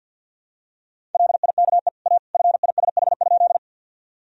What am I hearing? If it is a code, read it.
Morse code, heard as a keyed tone, about 46 wpm, 700 Hz. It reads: LIKE A FISH?
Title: Sentences from Top 1000 Words - Review Part 1 - 50wpm